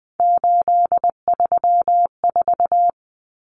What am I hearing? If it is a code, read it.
Morse code, heard as a keyed tone, about 20 wpm, 700 Hz. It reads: 834
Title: Common Cut Numbers - 3-Digit 20wpm